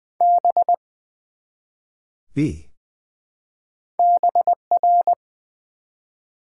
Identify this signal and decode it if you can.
Morse code, heard as a keyed tone, about 20 wpm, 700 Hz. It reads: B BR
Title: Build A Word - 1 Letter At A Time - 5-Letter Words - 20wpm